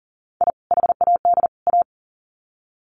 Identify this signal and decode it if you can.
Morse code, heard as a keyed tone, about 40 wpm, 700 Hz. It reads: I HAD A
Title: Sets of 3 Words 40wpm